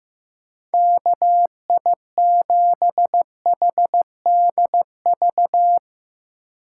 Morse code, heard as a keyed tone, about 16 wpm, 700 Hz. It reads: KI7HDV
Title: USA Call Signs - 6 Characters - 4x Character Spacing - 15wpm